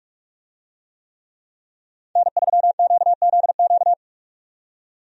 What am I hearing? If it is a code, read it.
Morse code, heard as a keyed tone, about 45 wpm, 700 Hz. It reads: N3QZQ
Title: USA Call Signs - 5 Characters - 45wpm